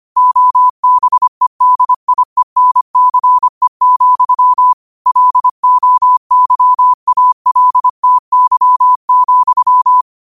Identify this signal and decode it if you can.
Morse code, heard as a keyed tone, about 25 wpm, 1000 Hz. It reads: OBEDIENCE, LOYALTY,